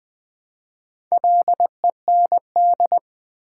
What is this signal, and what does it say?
Morse code, heard as a keyed tone, about 20 wpm, 700 Hz. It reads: LEND
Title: Build A Word - 2 Letters At A Time - 4-Letter Words - 20wpm